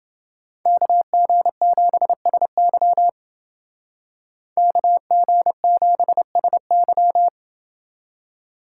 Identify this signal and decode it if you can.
Morse code, heard as a keyed tone, about 28 wpm, 700 Hz. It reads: KG7SY KG7SY